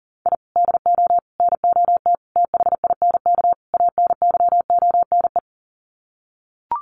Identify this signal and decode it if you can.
Morse code, heard as a keyed tone, about 40 wpm, 700 Hz. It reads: I DO NOT THINK ANYONE
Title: Sets of 5 Words Encore 40wpm